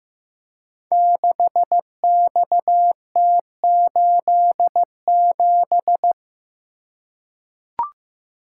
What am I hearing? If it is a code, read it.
Morse code, heard as a keyed tone, about 15 wpm, 700 Hz. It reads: 6XT87
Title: Code Groups - 5 Letter:Numbers - 15wpm